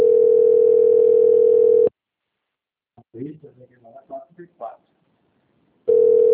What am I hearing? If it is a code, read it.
Ringback tone (North America), 440 + 480 Hz, with speech.
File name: rg-1600-unknown- - - .18449